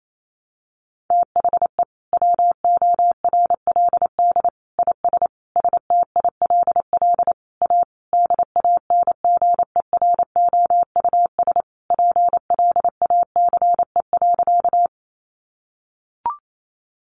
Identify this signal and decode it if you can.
Morse code, heard as a keyed tone, about 28 wpm, 700 Hz. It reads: THE WORLD IS STILL A DANGEROUS PLACE.
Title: Sentences from Top 1200 Words 28wpm